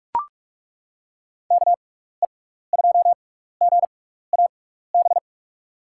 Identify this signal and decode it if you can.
Morse code, heard as a keyed tone, about 42 wpm, 700 Hz. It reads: K E 2 G A B
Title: USA Call Signs - 6 Characters - 6x Character Spacing - 45wpm